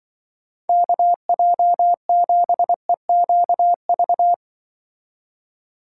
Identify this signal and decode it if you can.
Morse code, heard as a keyed tone, about 24 wpm, 700 Hz. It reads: KJ7EQV
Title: Speed Racing - USA 2x3 Call Signs 20wpm